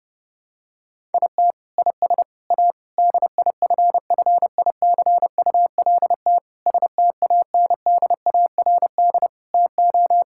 Morse code, heard as a keyed tone, about 32 wpm, 700 Hz. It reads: IT IS A DIFFICULT STANDARD TO
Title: Sentences from Top 1000 Words - Review Part 6 - 30wpm